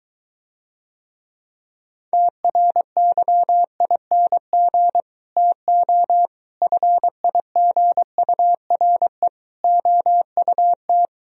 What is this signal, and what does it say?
Morse code, heard as a keyed tone, about 23 wpm, 700 Hz. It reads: TRYING TO FIGURE OUT